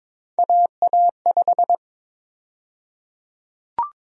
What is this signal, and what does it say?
Morse code, heard as a keyed tone, about 21 wpm, 700 Hz. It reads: AA5
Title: USA Call Sign Prefixes - 22wpm